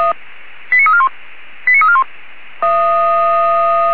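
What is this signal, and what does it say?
Five-tone selective call (ZVEI), heard as quick runs of five tones, tones of about 80 ms. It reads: 89341 89341
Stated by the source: ZVEI five-tone call received on SDR.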